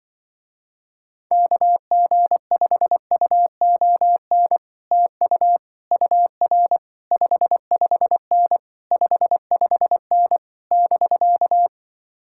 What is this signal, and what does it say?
Morse code, heard as a keyed tone, about 24 wpm, 700 Hz. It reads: KG5UON TU UR 55N 55N <BK>
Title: Speed-Racing Common QSO Phrases - 20wpm